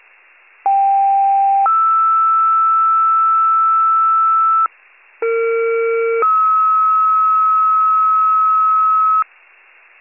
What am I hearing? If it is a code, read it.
Two-tone page, heard as a one-second tone, then a three-second tone, about 790 then 1320 Hz, about 460 then 1290 Hz. It is heard from 0.6 to 9.2 s.